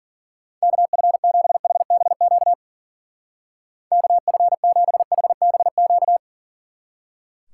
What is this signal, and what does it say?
Morse code, heard as a keyed tone, about 51 wpm, 700 Hz. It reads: KF7HBQ KF7HBQ